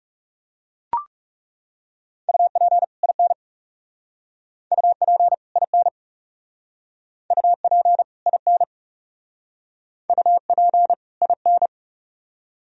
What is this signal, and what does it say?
Morse code, heard as a keyed tone, about 39 wpm, 700 Hz. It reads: UP IN UP IN UP IN UP IN